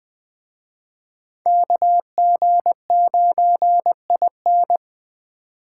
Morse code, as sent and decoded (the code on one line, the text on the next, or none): -.- --. ----. .. -.
KG9IN